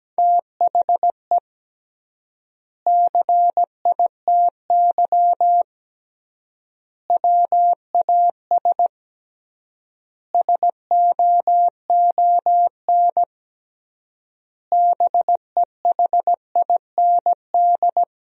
Morse code, heard as a keyed tone, about 17 wpm, 700 Hz. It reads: THE CITY WAS SOON BEHIND
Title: 3x Word-Spacing - Sentences from Top 500 Words - 17wpm